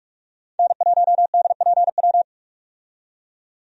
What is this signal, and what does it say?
Morse code, heard as a keyed tone, about 44 wpm, 700 Hz. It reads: N1DPW
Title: USA Call Signs - 5 Characters - 45wpm